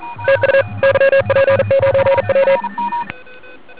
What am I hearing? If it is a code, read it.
Morse code, heard as a keyed tone, about 40 wpm, 530 Hz, with noise. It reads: TU YP9W
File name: ARCHIV / 40m cw1